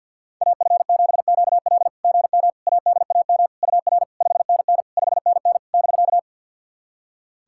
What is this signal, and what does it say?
Morse code, heard as a keyed tone, about 51 wpm, 690 Hz. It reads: AF7QL GM ADAM UR 5NN 5NN <BK>
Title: SOTA QSO Practice 50wpm